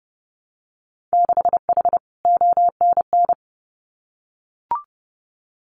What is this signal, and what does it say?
Morse code, heard as a keyed tone, about 30 wpm, 700 Hz. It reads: 6H ONN
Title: Winter Field Day Exchanges Encore 30wpm